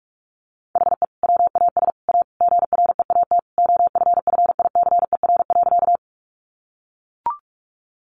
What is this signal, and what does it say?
Morse code, heard as a keyed tone, about 44 wpm, 700 Hz. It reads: HE WAS A GREAT OFFICER.
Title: Sentences from Top 1200 Words 45wpm